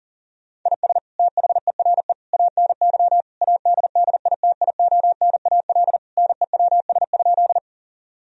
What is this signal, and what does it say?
Morse code, heard as a keyed tone, about 40 wpm, 700 Hz. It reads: IS THERE ANY ADDITIONAL NEWS?
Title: Sentences from Top 1000 Words - Review Part 4 - 40wpm